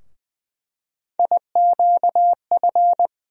Morse code, heard as a keyed tone, about 20 wpm, 700 Hz. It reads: IQF
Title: USA Call Sign Postfixes - 3 Letters 20wpm